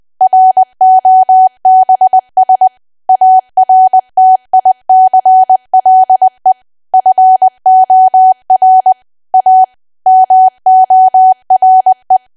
Morse code, heard as a keyed tone, about 20 wpm, 750 Hz. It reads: ROBS ARTICLE FOR A MORE
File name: cwa / W1AW-Level3 / 20WPM